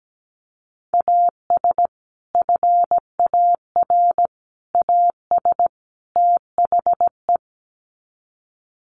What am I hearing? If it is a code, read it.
Morse code, heard as a keyed tone, about 17 wpm, 700 Hz. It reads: AS FAR AS THE
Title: Sets of 4 Words 17wpm